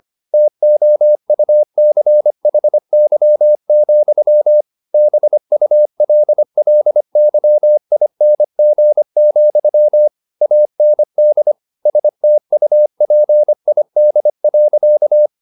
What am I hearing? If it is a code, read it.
Morse code, heard as a keyed tone, about 25 wpm, 600 Hz. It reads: TOUCHY, BULLYING, AND STUPID.
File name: Never 0025wpm